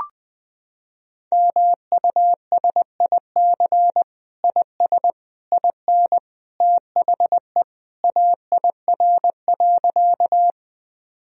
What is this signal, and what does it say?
Morse code, heard as a keyed tone, about 20 wpm, 700 Hz. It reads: MUSIC IS IN THE AIR.
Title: Sentences from Top 1000 Words - Review Part 1 - 20wpm